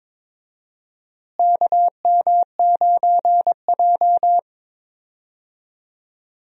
Morse code, heard as a keyed tone, about 22 wpm, 700 Hz. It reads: KM9J
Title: USA Call Signs - 4 Characters - Repeated 3x - 22wpm